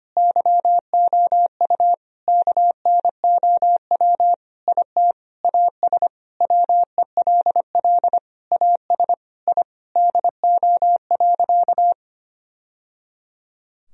Morse code, heard as a keyed tone, about 25 wpm, 700 Hz. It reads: YOU KNOW IT AS WELL AS I DO.
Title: Sentences from Top 1000 Words - Review Part 3 - 25wpm